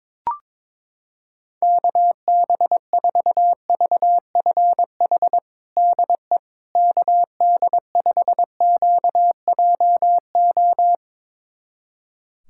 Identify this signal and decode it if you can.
Morse code, heard as a keyed tone, about 21 wpm, 700 Hz. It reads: KB4VFH DE KD5QJO
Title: USA Callsign DE Callsign 22wpm